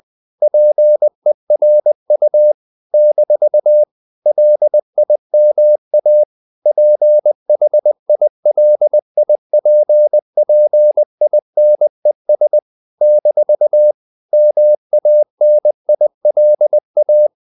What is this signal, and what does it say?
Morse code, heard as a keyed tone, about 20 wpm, 600 Hz. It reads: PERU - LIMA PHILIPPINES - MANILA
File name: CW / Practice / NationalCapitals20